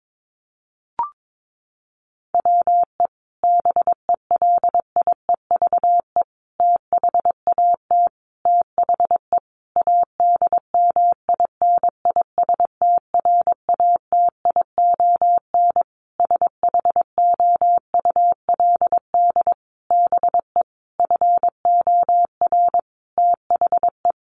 Morse code, heard as a keyed tone, about 22 wpm, 700 Hz. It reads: WE BELIEVE THAT THE ADMINISTRATION SHOULD BE FOR THE
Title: Sentences from Top 1300 Words 22wpm